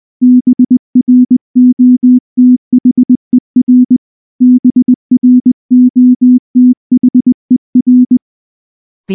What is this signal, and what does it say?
Morse code, heard as a keyed tone, about 19 wpm, 250 Hz. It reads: BROTHER BROTHER
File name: morse / morse-training-20WPM-8